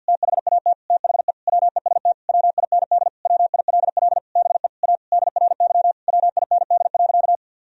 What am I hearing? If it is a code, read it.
Morse code, heard as a keyed tone, about 52 wpm, 700 Hz. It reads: THAT THE WEST WIND WILL BE A DRY WIND.
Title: Sentences from Top 1000 Words - Review Part 3 - 50wpm